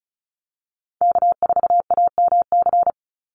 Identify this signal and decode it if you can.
Morse code, heard as a keyed tone, about 34 wpm, 700 Hz. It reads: K4AMC
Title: Call Signs Repeated 2x - 35wpm